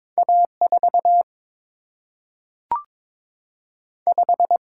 Morse code, heard as a keyed tone, about 22 wpm, 700 Hz. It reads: A 4 5
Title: Single-Letter Single-Number 22wpm